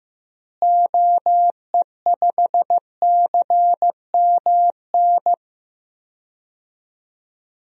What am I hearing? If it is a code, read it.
Morse code, heard as a keyed tone, about 15 wpm, 700 Hz. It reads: OE5CMN
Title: DXCC-100 Call Signs - 6 Characters - Repeated 2x - 15wpm